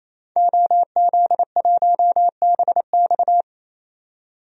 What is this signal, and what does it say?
Morse code, heard as a keyed tone, about 28 wpm, 700 Hz. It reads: OZ1BX
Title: DXCC-100 Call Signs - 5 Characters - 28wpm